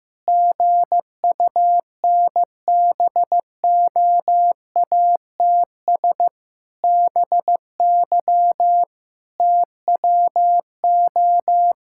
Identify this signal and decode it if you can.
Morse code, heard as a keyed tone, about 15 wpm, 700 Hz. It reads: GUNBOATS BY TWO
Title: Pangram Challenge v2 - 15wpm